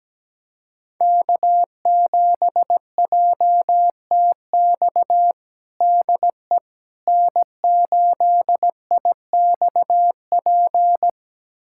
Morse code, heard as a keyed tone, about 17 wpm, 700 Hz. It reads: K7JTX DE N8IXP
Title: USA Callsign DE Callsign - 5 Characters - 17wpm